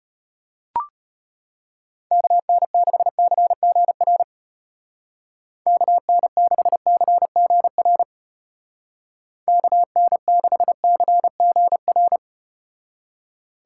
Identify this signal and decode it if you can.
Morse code, heard as a keyed tone, about 34 wpm, 700 Hz. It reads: KN6CGR KN6CGR KN6CGR